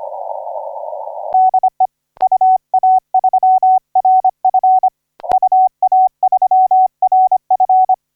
Morse code, heard as a keyed tone, about 25 wpm, 750 Hz. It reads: DE UA3RF UA3RF
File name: DU1WGA 0838UTC 15m